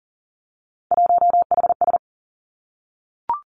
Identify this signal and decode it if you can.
Morse code, heard as a keyed tone, about 38 wpm, 700 Hz. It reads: 1HS
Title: Three Challenging Letter-Number Combinations 40wpm